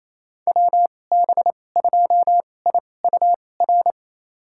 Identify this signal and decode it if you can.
Morse code, heard as a keyed tone, about 28 wpm, spaced as for 20 wpm, 700 Hz. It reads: WB2IUR
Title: USA Call Signs - 6 Characters - 2x Character Spacing - 28wpm